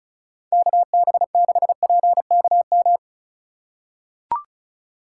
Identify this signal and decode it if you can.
Morse code, heard as a keyed tone, about 36 wpm, 700 Hz. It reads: KB6PKM
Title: USA Call Signs - 2x Character Spacing - 35wpm